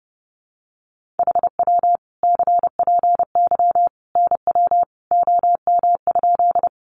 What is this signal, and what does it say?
Morse code, heard as a keyed tone, about 30 wpm, 700 Hz. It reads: HW CPY NW OM?
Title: Common QSO Phrases 30wpm